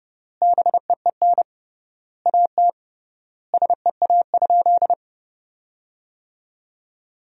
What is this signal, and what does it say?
Morse code, heard as a keyed tone, about 30 wpm, 700 Hz. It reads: BEEN AT SEA?
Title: 3x Word-Spacing - Sentences from Top 300 Words - 30wpm